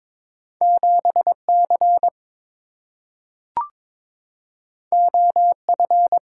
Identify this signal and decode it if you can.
Morse code, heard as a keyed tone, about 22 wpm, 700 Hz. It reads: 7C OF
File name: practice / 22wpm / Two Letter-Number Combinations - 22wpm